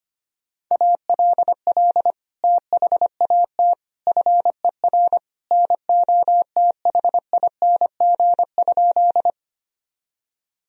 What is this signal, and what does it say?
Morse code, heard as a keyed tone, about 25 wpm, 700 Hz. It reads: ALL THAT FER NOTHING?
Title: Sentences from Top 500 Words and Abbreviations 25wpm